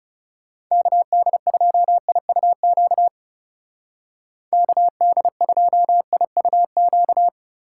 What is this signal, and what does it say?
Morse code, heard as a keyed tone, about 37 wpm, 700 Hz. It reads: KD2IUQ KD2IUQ
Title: Speed Racing - USA 2x3 Call Signs 30wpm